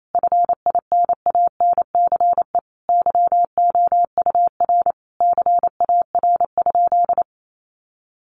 Morse code, heard as a keyed tone, about 28 wpm, 700 Hz. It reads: FINANCE YOUR CAR?
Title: Sentences from Top 1500 Words 28wpm